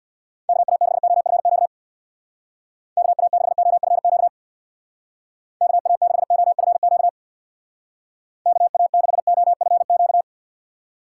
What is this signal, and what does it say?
Morse code, heard as a keyed tone, about 60 wpm, 700 Hz. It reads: KA6OFQ KA6OFQ KA6OFQ KA6OFQ
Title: Speed Racing - USA 2x3 Call Signs 50wpm